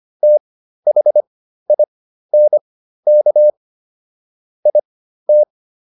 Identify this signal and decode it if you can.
Morse code, heard as a keyed tone, about 25 wpm, spaced as for 13 wpm, 600 Hz. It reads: THINK IT